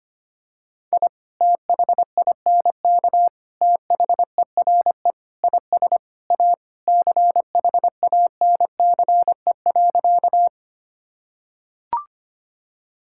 Morse code, heard as a keyed tone, about 25 wpm, 700 Hz. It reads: I THINK THERE IS A CHANCE.
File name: practice / 25wpm / Sentences with 3-Word Phrases - 25wpm